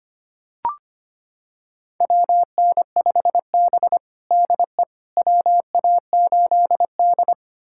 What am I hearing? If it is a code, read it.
Morse code, heard as a keyed tone, about 25 wpm, 700 Hz. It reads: WN5B DE WA8D
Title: USA Callsign DE Callsign - 4 Characters - 25wpm